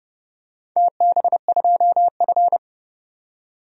Morse code, heard as a keyed tone, about 30 wpm, 700 Hz. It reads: TB2F
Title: DXCC-100 Call Signs - 4 Characters - 30wpm